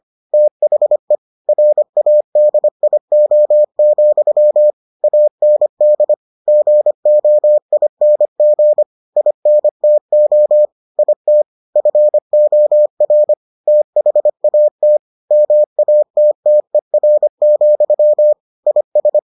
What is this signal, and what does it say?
Morse code, heard as a keyed tone, about 25 wpm, 600 Hz. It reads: THE RADIO, AND GOING INTO IT FOR THAT MATTER, IS